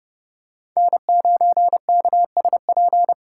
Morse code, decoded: N9KSP

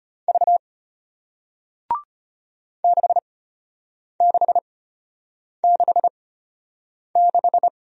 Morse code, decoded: V 6666

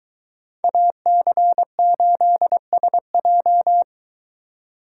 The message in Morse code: AC8SJ

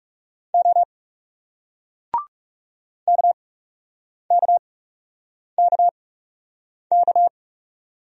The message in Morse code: O KKKK

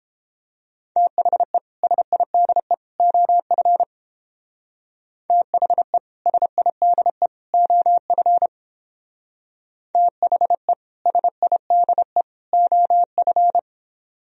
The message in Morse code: THE SIDE OF THE SIDE OF THE SIDE OF

